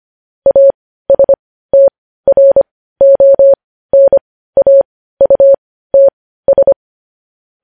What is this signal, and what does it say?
Morse code, heard as a keyed tone, about 25 wpm, spaced as for 15 wpm, 550 Hz. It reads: ASTRONAUTS
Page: news headlines in morse code at 15 wpm